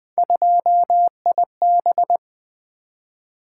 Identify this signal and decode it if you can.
Morse code, heard as a keyed tone, about 20 wpm, 700 Hz. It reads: 2IB
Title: Three Letter-Number Combinations - 20wpm